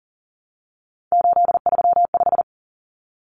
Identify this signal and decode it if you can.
Morse code, heard as a keyed tone, about 40 wpm, 700 Hz. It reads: 835
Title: Numbers - 3-Digit 40wpm